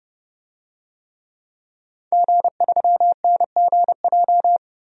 Morse code, heard as a keyed tone, about 30 wpm, 700 Hz. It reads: G3NGJ
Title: DXCC-100 Call Signs - 5 Characters - 30wpm